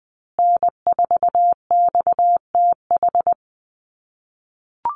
Morse code, heard as a keyed tone, about 20 wpm, 700 Hz. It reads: N4XTH
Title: USA Call Signs Repeated 3x 20wpm